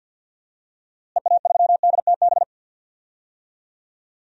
Morse code, heard as a keyed tone, about 46 wpm, 700 Hz. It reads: EA3DTB
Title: DXCC-100 Call Signs - 6 Characters - Repeated 3x - 50wpm